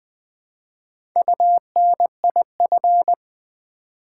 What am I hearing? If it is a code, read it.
Morse code, heard as a keyed tone, about 20 wpm, 700 Hz. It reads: UNIF
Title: Build A Word - 1 Letter At A Time - 3 to 5-Letter Words - 20wpm